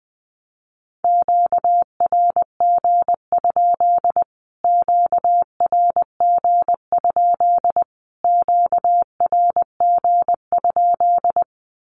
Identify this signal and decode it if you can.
Morse code, heard as a keyed tone, about 20 wpm, 700 Hz. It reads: QRG? QRG? QRG?
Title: Q-Codes Repeated 3x 20wpm